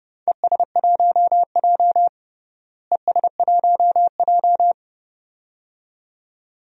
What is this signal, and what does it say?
Morse code, heard as a keyed tone, about 30 wpm, 700 Hz. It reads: ES1J ES1J
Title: DXCC-100 Call Signs - 4 Characters - Repeated 2x - 30wpm